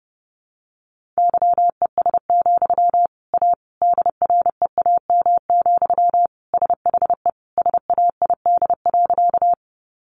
Morse code, heard as a keyed tone, about 30 wpm, 700 Hz. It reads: YES, A DREAM, SHE SAID.
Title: Sentences from Top 1500 Words 30wpm